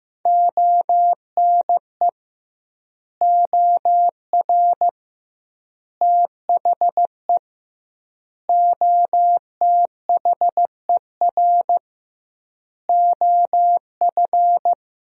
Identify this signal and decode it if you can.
Morse code, heard as a keyed tone, about 15 wpm, 700 Hz. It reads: ONE OR THE OTHER OF
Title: 2x Word-Spacing - Sentences from Top 400 Words - 15wpm